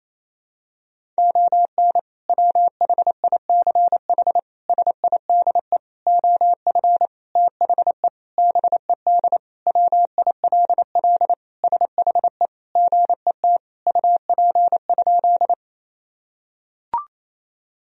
Morse code, heard as a keyed tone, about 28 wpm, 700 Hz. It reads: ON WHICH SIDE OF THE BED WILL SHE GET UP?
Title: Sentences from Top 1000 Words - Review Part 1 - 28wpm